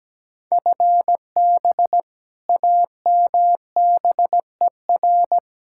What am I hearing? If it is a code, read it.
Morse code, heard as a keyed tone, about 17 wpm, 700 Hz. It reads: FB AMBER
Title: Practice QSO - Exchange 3 - More Info - 17wpm